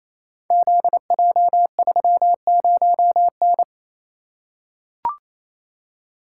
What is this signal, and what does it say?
Morse code, heard as a keyed tone, about 27 wpm, 700 Hz. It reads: ZJ30N
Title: Code Groups - 5 Letter:Numbers - 28wpm